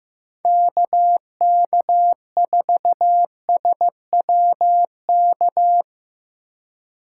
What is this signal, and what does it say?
Morse code, heard as a keyed tone, about 17 wpm, 700 Hz. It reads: KK4SWK
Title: Speed Racing - USA 2x3 Call Signs 15wpm